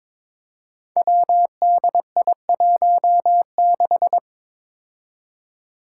Morse code, heard as a keyed tone, about 22 wpm, 700 Hz. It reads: WDI16
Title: Five Challenging Letter-Number Combinations 22wpm